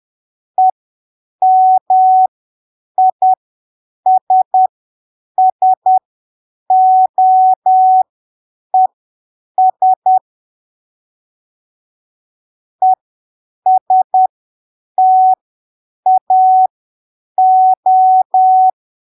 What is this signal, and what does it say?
Morse code, heard as a keyed tone, about 10 wpm, spaced as for 7 wpm, 750 Hz. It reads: EMISSOES ESTAO
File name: arquivos / texto / texto 01 10ppm